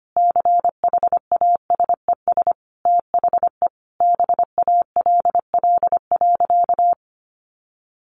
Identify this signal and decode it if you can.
Morse code, heard as a keyed tone, about 25 wpm, 700 Hz. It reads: CHASES THE BALL.
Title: Additional Easy Phrases 25wpm